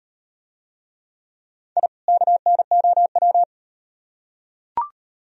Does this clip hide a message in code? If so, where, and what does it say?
Yes: Morse code from 1.8 to 3.4 s, heard as a keyed tone, about 38 wpm, 700 Hz. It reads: I KNOW